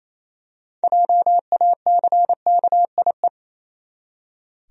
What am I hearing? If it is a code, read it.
Morse code, heard as a keyed tone, about 27 wpm, 700 Hz. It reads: JACKIE
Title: USA Common Names - 28wpm